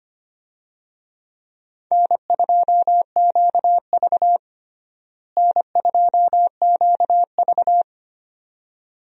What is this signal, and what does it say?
Morse code, heard as a keyed tone, about 25 wpm, 700 Hz. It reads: N2QV N2QV
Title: USA Call Signs - 4 Characters - Repeated 2x - 25wpm